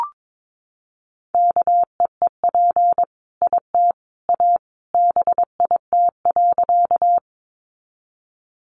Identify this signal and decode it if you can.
Morse code, heard as a keyed tone, about 22 wpm, 700 Hz. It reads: KEEP IT A BIT.
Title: Sentences from Top 900 Words 22wpm